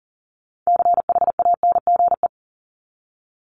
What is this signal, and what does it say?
Morse code, heard as a keyed tone, about 40 wpm, 700 Hz. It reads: CHANGE
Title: Top 500 Words 40wpm